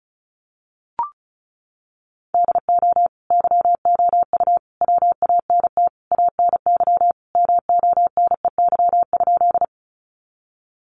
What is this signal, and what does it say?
Morse code, heard as a keyed tone, about 34 wpm, 700 Hz. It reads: DO YOU WANT ANY MONEY?
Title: Sentences from Top 400 Words 35wpm